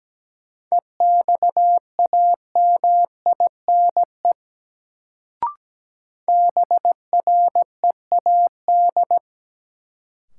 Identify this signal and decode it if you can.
Morse code, heard as a keyed tone, about 17 wpm, 700 Hz. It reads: EXAMINE BREAD
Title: Sentences from Top 1600 Words - New Word Drill - 17wpm